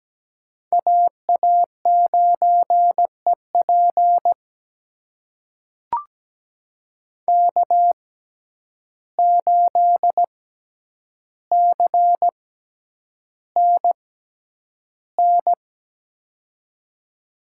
Morse code, heard as a keyed tone, about 17 wpm, spaced as for 9 wpm, 700 Hz. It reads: AA9EP K8CNN
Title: USA Call Signs - 5 Characters - 6x Character Spacing - 17wpm